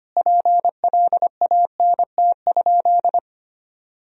Morse code, heard as a keyed tone, about 25 wpm, 700 Hz. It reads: PLANT?
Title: Sentences from Top 1000 Words - Review Part 1 - 25wpm